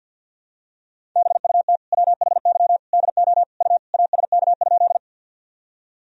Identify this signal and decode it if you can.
Morse code, heard as a keyed tone, about 52 wpm, 700 Hz. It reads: BUT WHY DO U ASK?
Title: Sentences from Top 200 Words and Abbreviations 50wpm